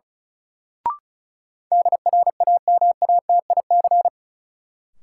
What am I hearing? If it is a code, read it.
Morse code, heard as a keyed tone, about 34 wpm, 700 Hz. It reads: DRAMATIC
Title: Sentences from Top 1300 Words - New Word Drill Alt - 35wpm